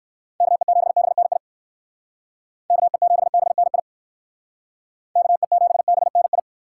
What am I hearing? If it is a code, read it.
Morse code, heard as a keyed tone, about 58 wpm, 700 Hz. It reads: KE7BNI KE7BNI KE7BNI